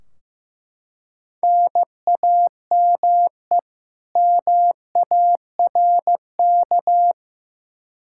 Morse code, heard as a keyed tone, about 15 wpm, 700 Hz. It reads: NAME MARK
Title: Practice QSO - Exchange 1 - Beginning - 15wpm